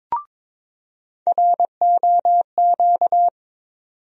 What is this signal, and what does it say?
Morse code, heard as a keyed tone, about 22 wpm, 700 Hz. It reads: ROQ
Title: USA Call Sign Postfixes - 3 Letters 22wpm